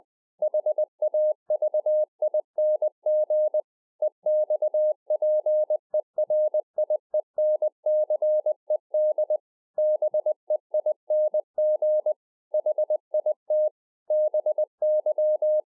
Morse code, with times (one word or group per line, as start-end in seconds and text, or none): 0.4-3.6 HAVING
4.0-9.4 EXPERIENCED
9.8-12.1 BEING
12.5-13.7 HIT
14.1-15.6 BY